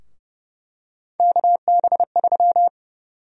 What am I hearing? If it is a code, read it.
Morse code, heard as a keyed tone, about 31 wpm, 700 Hz. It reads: KB3
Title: USA Call Sign Prefixes - 30wpm